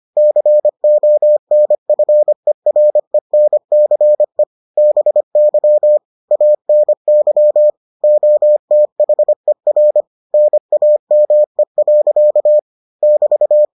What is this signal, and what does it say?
Morse code, heard as a keyed tone, about 25 wpm, 600 Hz. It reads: CONFERENCE BY ANY OTHER NAME. <BT>